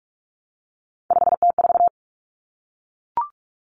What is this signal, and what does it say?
Morse code, heard as a keyed tone, about 44 wpm, 700 Hz. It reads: 5T4
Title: Common Cut Numbers - 3-Digit 45wpm